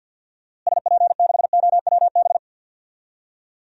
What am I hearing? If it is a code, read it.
Morse code, heard as a keyed tone, about 52 wpm, 700 Hz. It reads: SP6OWB
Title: DXCC-100 Call Signs - 6 Characters - Repeated 2x - 50wpm